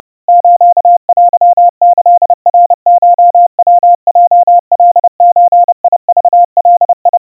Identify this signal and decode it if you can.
Morse code (Japanese Wabun code), heard as a keyed tone, about 30 wpm, 700 Hz. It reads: ステキナコヤヲカゾクガ
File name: kawa / Morse sample / kamyu 110 cpm 700 Hz Crispy